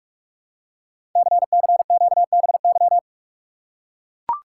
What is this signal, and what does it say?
Morse code, heard as a keyed tone, about 44 wpm, 690 Hz. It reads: CCQBY
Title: Code Groups - 5 Letters - 45wpm